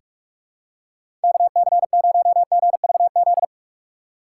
Morse code, heard as a keyed tone, about 46 wpm, 700 Hz. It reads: KC0GVZ
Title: USA Call Signs - 6 Characters - 3x Character Spacing - 45wpm